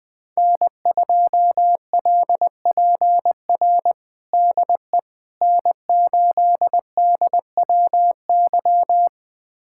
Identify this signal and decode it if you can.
Morse code, heard as a keyed tone, about 20 wpm, 700 Hz. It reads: N2LPR DE N8DWY
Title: USA Callsign DE Callsign - 5 Characters - 20wpm